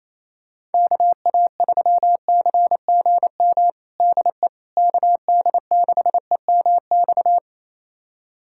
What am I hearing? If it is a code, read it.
Morse code, heard as a keyed tone, about 28 wpm, 700 Hz. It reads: KA3CGM DE KD6EMX
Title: USA Callsign DE Callsign 28wpm